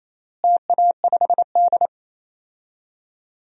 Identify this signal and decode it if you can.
Morse code, heard as a keyed tone, about 28 wpm, 700 Hz. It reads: TA5D
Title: DXCC-100 Call Signs - 4 Characters - Repeated 2x - 28wpm